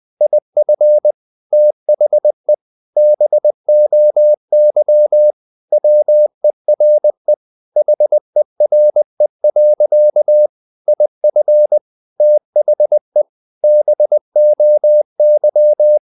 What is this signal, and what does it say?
Morse code, heard as a keyed tone, about 20 wpm, 600 Hz. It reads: IF THE BOY WERE HERE. IF THE BOY
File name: the old man and the sea part 2 20wpm